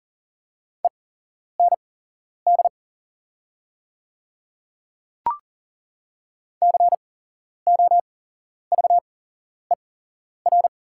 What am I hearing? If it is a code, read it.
Morse code, heard as a keyed tone, about 40 wpm, spaced as for 11 wpm, 690 Hz. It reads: END COVER
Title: 8x Character Spacing - Top 500 Words - 40wpm